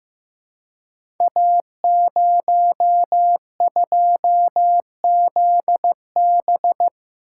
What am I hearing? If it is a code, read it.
Morse code, heard as a keyed tone, about 15 wpm, 700 Hz. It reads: A02ZB
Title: Code Groups - 5 Letter:Numbers - 15wpm